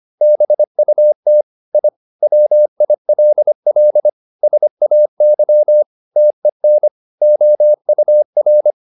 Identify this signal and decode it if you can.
Morse code, heard as a keyed tone, about 25 wpm, 600 Hz. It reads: BUT I WILL SAY TEN OUR
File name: the old man and the sea part 1 25wpm